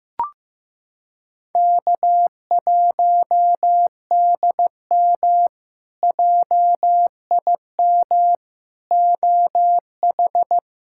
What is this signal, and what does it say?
Morse code, heard as a keyed tone, about 15 wpm, 700 Hz. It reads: K1DM JIM OH
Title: K1USN SST QSO Practice - 15wpm